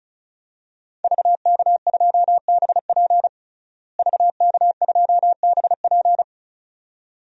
Morse code, heard as a keyed tone, about 34 wpm, 700 Hz. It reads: VK2BP VK2BP
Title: DXCC-100 Call Signs - 5 Characters - Repeated 2x - 35wpm